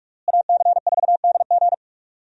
Morse code, heard as a keyed tone, about 47 wpm, 700 Hz. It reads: AC4DG
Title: USA Call Signs - 5 Characters - Repeated 3x - 45wpm